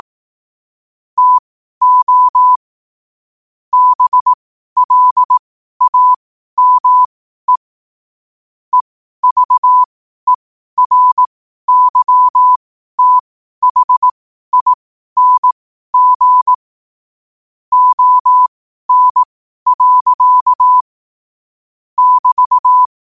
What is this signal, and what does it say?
Morse code, heard as a keyed tone, about 18 wpm, spaced as for 12 wpm, 1000 Hz. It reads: TO BLAME EVERYTHING ON. <BT>
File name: qotdfiles / 12wpm